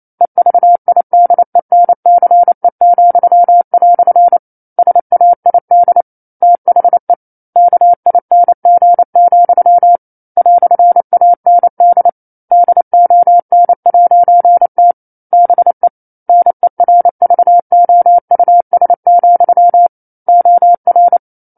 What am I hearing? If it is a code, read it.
Morse code, heard as a keyed tone, about 28 wpm, 700 Hz. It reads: EVIDENCE," SAID THE KING, "AND DON'T BE NERVOUS, OR
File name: 09-Chapter-09 28 WPM